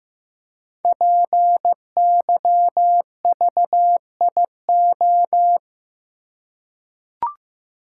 Morse code, heard as a keyed tone, about 15 wpm, 700 Hz. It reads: PYVIO
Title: Code Groups - 5 Letter:Numbers - 15wpm